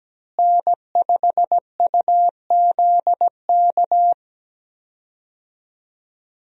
Morse code, heard as a keyed tone, about 17 wpm, 700 Hz. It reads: N5UZK
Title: USA Call Signs - 5 Characters - Repeated 2x - 17wpm